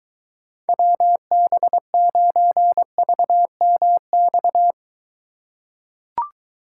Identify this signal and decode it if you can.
Morse code, heard as a keyed tone, about 23 wpm, 700 Hz. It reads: WB9VMX